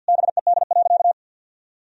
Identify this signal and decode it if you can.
Morse code, heard as a keyed tone, about 51 wpm, 700 Hz. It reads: BED.
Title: Sentences from Top 1000 Words - Review Part 1 - 50wpm